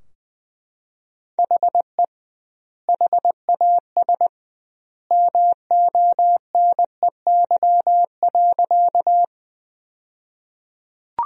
Morse code, heard as a keyed tone, about 20 wpm, 700 Hz. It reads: HE HAS MONEY.
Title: 2x Word-Spacing - Sentences from Top 400 Words - 20wpm